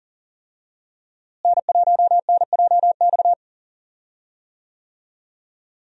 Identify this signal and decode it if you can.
Morse code, heard as a keyed tone, about 40 wpm, 700 Hz. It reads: N1NJX
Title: USA Call Signs - 5 Characters - Repeated 2x - 40wpm